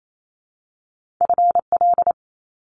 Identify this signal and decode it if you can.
Morse code, heard as a keyed tone, about 28 wpm, 700 Hz. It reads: FL
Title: Top Two Letter Combinations in Words 28wpm